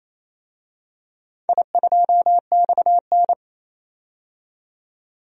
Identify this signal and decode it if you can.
Morse code, heard as a keyed tone, about 29 wpm, 700 Hz. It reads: I2XN
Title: DXCC-100 Call Signs - 4 Characters - 28wpm